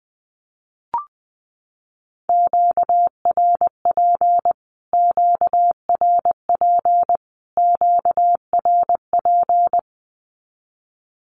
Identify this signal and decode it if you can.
Morse code, heard as a keyed tone, about 20 wpm, 700 Hz. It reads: QRP QRP QRP
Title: Q-Codes Repeated 3x 20wpm